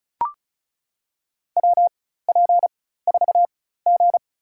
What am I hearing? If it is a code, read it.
Morse code, heard as a keyed tone, about 36 wpm, spaced as for 16 wpm, 690 Hz. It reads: WP4G
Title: USA Call Signs - 4x Character Spacing - 35wpm